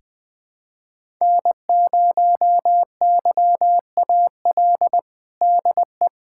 Morse code, heard as a keyed tone, about 20 wpm, 700 Hz. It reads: N0YAL DE
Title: USA Callsign DE Callsign - 5 Characters - 20wpm